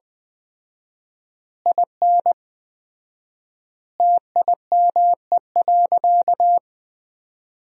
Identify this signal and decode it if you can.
Morse code, heard as a keyed tone, about 21 wpm, 700 Hz. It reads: IN TIME.
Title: 4x Word-Spacing - Sentences from Top 300 Words - 20wpm